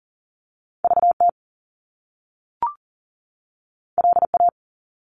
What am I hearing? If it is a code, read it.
Morse code, heard as a keyed tone, about 42 wpm, 700 Hz. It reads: VT LA
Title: US States 40wpm